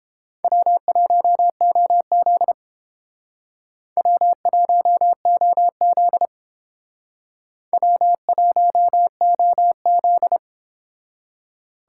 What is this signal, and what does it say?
Morse code, heard as a keyed tone, about 28 wpm, 700 Hz. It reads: W1OZ W1OZ W1OZ